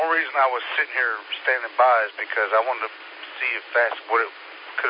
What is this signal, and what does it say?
Emergency Alert System (SAME) tones, heard as harsh buzzing bursts, with speech. It is heard from 0.8 to 3.7 s.